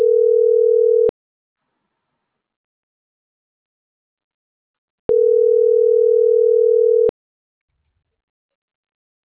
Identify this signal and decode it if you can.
Ringback tone (North America), 440 + 480 Hz.